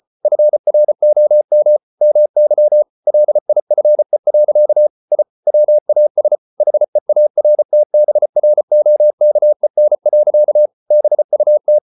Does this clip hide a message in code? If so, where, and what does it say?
Yes: Morse code from 0.2 to 11.8 s, heard as a keyed tone, about 34 wpm, 600 Hz. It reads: FROM MY LIFE. I WAS HEARTBROKEN. BUT